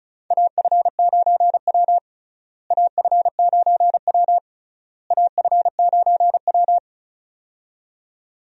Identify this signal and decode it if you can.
Morse code, heard as a keyed tone, about 36 wpm, 700 Hz. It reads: AF9W AF9W AF9W
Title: USA Call Signs - 4 Characters - Repeated 3x - 35wpm